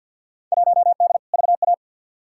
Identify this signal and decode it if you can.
Morse code, heard as a keyed tone, about 52 wpm, 700 Hz. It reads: 1D VA
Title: Field Day Exchanges - 50wpm